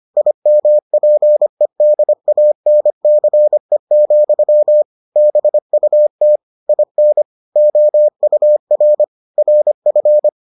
Morse code, heard as a keyed tone, about 25 wpm, 600 Hz. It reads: IMPEDANCE, BUT IN OUR RF